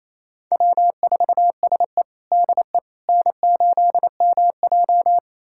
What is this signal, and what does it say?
Morse code, heard as a keyed tone, about 28 wpm, 700 Hz. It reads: W4SE DE N8MJ
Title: USA Callsign DE Callsign - 4 Characters - 28wpm